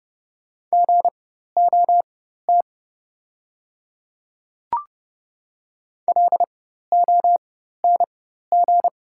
Morse code, heard as a keyed tone, about 30 wpm, spaced as for 14 wpm, 690 Hz. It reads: GOT LONG
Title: 4x Character Spacing - Top 300 Words - 30wpm